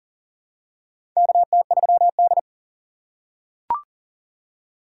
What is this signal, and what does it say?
Morse code, heard as a keyed tone, about 37 wpm, 700 Hz. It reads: KT3D
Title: USA Call Signs - 4 Characters - 4x Character Spacing - 40wpm